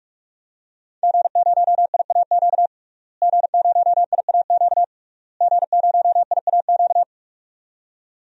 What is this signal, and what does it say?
Morse code, heard as a keyed tone, about 46 wpm, 700 Hz. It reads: G0IAQ G0IAQ G0IAQ
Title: DXCC-100 Call Signs - 5 Characters - Repeated 3x - 45wpm